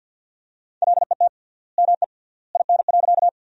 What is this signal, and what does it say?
Morse code, heard as a keyed tone, about 50 wpm, 700 Hz. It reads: LET ME IN.
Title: 3x Word-Spacing - Sentences from Top 300 Words - 50wpm